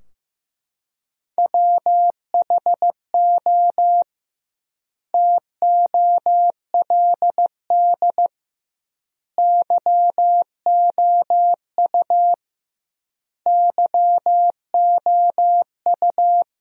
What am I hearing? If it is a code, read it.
Morse code, heard as a keyed tone, about 15 wpm, 700 Hz. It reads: WHO TOLD YOU YOU
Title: 2x Word-Spacing - Sentences from Top 400 Words - 15wpm